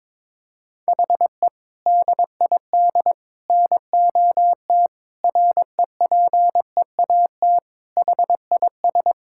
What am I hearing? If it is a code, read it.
Morse code, heard as a keyed tone, about 22 wpm, 700 Hz. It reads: HE DID NOT REPEAT HIS
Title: Sentences from Top 1600 Words 22wpm